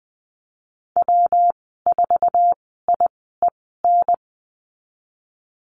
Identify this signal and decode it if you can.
Morse code, heard as a keyed tone, about 20 wpm, spaced as for 15 wpm, 700 Hz. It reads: W4IEN
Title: 179 Lesson 29g - Slash - Call Signs 20wpm